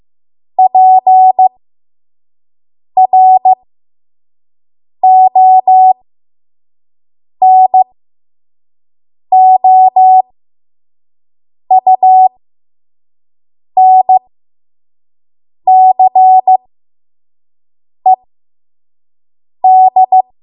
Morse code, heard as a keyed tone, about 15 wpm, spaced as for 5 wpm, 750 Hz. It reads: PRONOUNCED
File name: Morse / 05WPM